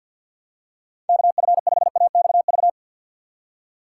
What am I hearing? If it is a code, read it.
Morse code, heard as a keyed tone, about 50 wpm, 690 Hz. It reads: KF5AXV
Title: USA Call Signs - 6 Characters - 3x Character Spacing - 50wpm